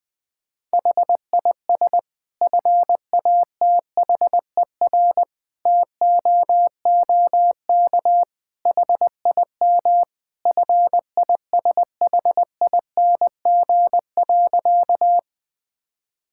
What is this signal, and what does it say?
Morse code, heard as a keyed tone, about 19 wpm, 700 Hz. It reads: HIS FATHER TOOK HIM FISHING.
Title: Sentences from Top 1000 Words - Review Part 5 - 20wpm